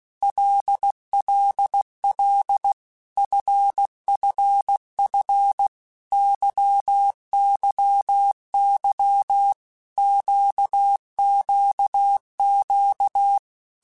Morse code, heard as a keyed tone, about 16 wpm, 800 Hz. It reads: LLL FFF YYY QQQ
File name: Basic Letter Groups By7Groups / LFCYQ / LFCYQ@15